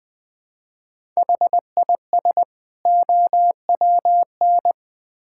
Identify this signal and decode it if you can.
Morse code, heard as a keyed tone, about 20 wpm, 700 Hz. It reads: HIS OWN
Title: Build A Phrase - 2-Word Phrases - 20wpm